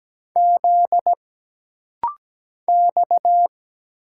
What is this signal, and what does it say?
Morse code, heard as a keyed tone, about 17 wpm, 700 Hz. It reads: Z X
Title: Single Letters - Rapid-Fire 17wpm